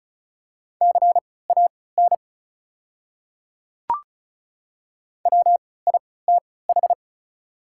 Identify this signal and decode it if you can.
Morse code, heard as a keyed tone, about 36 wpm, spaced as for 20 wpm, 690 Hz. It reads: CAN WITH
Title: 3x Character Spacing - Top 100 Words - 35wpm